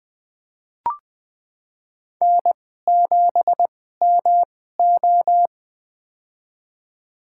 Morse code, heard as a keyed tone, about 20 wpm, spaced as for 14 wpm, 700 Hz. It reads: N7MO
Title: USA Call Signs - 2x Character Spacing - 20wpm